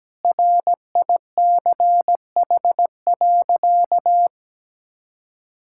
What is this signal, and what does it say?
Morse code, heard as a keyed tone, about 17 wpm, 700 Hz. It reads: RICH.
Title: Sentences from Top 1600 Words 17wpm